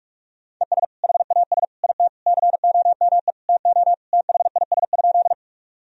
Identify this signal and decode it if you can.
Morse code, heard as a keyed tone, about 41 wpm, 700 Hz. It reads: ES HAS IT COME TO THIS?
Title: Sentences from Top 100 Words and Abbreviations 45wpm